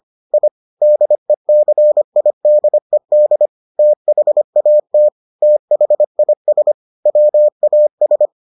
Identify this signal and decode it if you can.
Morse code, heard as a keyed tone, about 25 wpm, 600 Hz. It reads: I DECIDED THAT THIS WAS